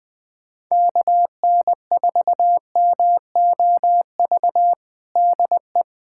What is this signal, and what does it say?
Morse code, heard as a keyed tone, about 20 wpm, 700 Hz. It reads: KN4MOV DE
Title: USA Callsign DE Callsign - 6 Characters - 20wpm